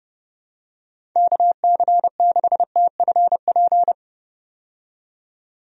Morse code, heard as a keyed tone, about 30 wpm, 700 Hz. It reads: KC6TFP
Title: USA Call Signs - 6 Characters - 30wpm